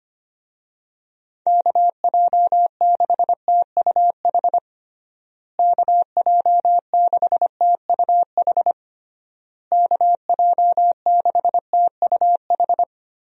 Morse code, heard as a keyed tone, about 25 wpm, 700 Hz. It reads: KJ6TUH KJ6TUH KJ6TUH
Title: USA Call Signs - 6 Characters - Repeated 3x - 25wpm